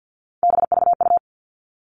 Morse code, heard as a keyed tone, about 50 wpm, 700 Hz. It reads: BVU
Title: All Cut Numbers - 3-Digit 50wpm